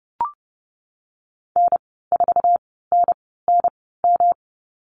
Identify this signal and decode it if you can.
Morse code, heard as a keyed tone, about 30 wpm, spaced as for 17 wpm, 700 Hz. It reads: N4NNM
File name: Lesson 19g - Letter w - Call Signs 30wpm